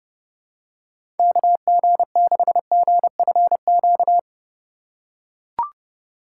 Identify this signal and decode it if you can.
Morse code, heard as a keyed tone, about 32 wpm, 700 Hz. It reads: KG6GFQ
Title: USA Call Signs - 6 Characters - 4x Character Spacing - 30wpm